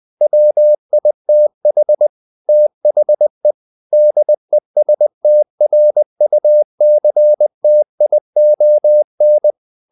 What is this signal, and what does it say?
Morse code, heard as a keyed tone, about 20 wpm, 600 Hz. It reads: WITH THE DESTRUCTION
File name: the old man and the sea part 2 20wpm